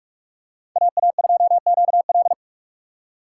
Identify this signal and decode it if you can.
Morse code, heard as a keyed tone, about 44 wpm, 700 Hz. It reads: AA2QL
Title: USA Call Signs - 5 Characters - 4x Character Spacing - 45wpm